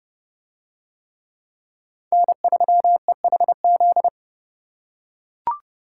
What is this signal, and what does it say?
Morse code, heard as a keyed tone, about 30 wpm, 700 Hz. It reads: N3EHZ